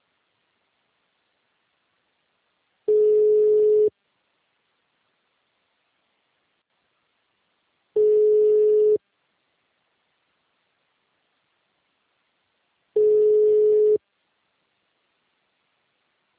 Ringback tone (Europe, 425 Hz).
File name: out- -2006- - - .5468